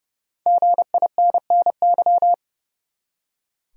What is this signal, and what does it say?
Morse code, heard as a keyed tone, about 30 wpm, 700 Hz. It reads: GINNY
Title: USA Common Names - Max 5 Letters - 30wpm